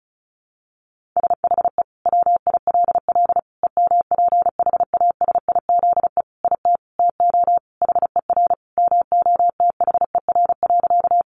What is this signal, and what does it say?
Morse code, heard as a keyed tone, about 34 wpm, 700 Hz. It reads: SHE WILL EMPHASIZE IT TO HER MOTHER.
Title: Sentences from Top 1500 Words 35wpm